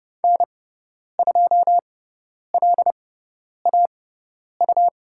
Morse code, heard as a keyed tone, about 30 wpm, spaced as for 10 wpm, 700 Hz. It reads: N2LAU
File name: Lesson 18g - Letter m - Call Signs 30wpm